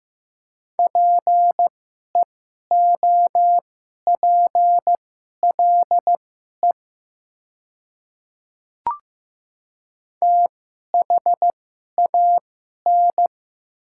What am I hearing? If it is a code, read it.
Morse code, heard as a keyed tone, about 15 wpm, spaced as for 11 wpm, 690 Hz. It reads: PEOPLE THAN
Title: 2x Character Spacing - Top 100 Words - 15wpm